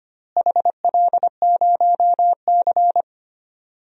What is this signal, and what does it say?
Morse code, heard as a keyed tone, about 25 wpm, 700 Hz. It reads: HL0C
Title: DXCC-100 Call Signs - 4 Characters - Repeated 2x - 25wpm